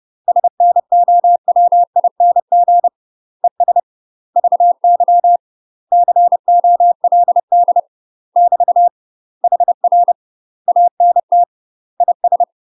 Morse code, heard as a keyed tone, about 30 wpm, 700 Hz. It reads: SNOWING ES VY COLD <BT> HR ANT IS
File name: UK MASTER QSOs TRAINING / 30WPM / UK MASTER QSOs 0044